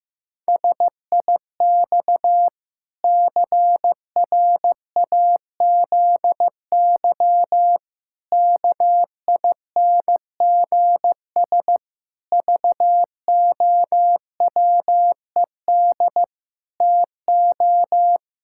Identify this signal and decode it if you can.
Morse code, heard as a keyed tone, about 15 wpm, 700 Hz. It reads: SIX CRAZY KINGS VOWED TO
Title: Pangram Challenge v2 - 15wpm